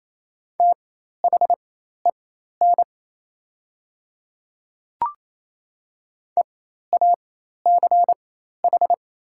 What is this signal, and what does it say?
Morse code, heard as a keyed tone, about 28 wpm, spaced as for 13 wpm, 700 Hz. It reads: THEN EACH
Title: 4x Character Spacing - Top 200 Words - 28wpm